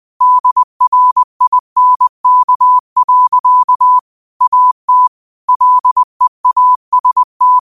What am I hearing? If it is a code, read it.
Morse code, heard as a keyed tone, about 20 wpm, 1000 Hz. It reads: DRINK. AT LEAST
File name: qotdfiles / 20wpm